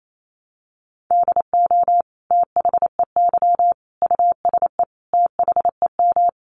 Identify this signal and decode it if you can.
Morse code, heard as a keyed tone, about 28 wpm, 700 Hz. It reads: DO THEY USE THEM
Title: Sentences from Top 1300 Words 28wpm